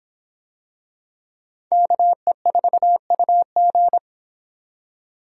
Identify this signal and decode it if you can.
Morse code, heard as a keyed tone, about 25 wpm, 700 Hz. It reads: KE4UG